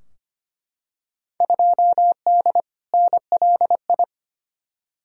Morse code, heard as a keyed tone, about 25 wpm, 700 Hz. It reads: 2D NLI
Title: Field Day Exchanges - 25wpm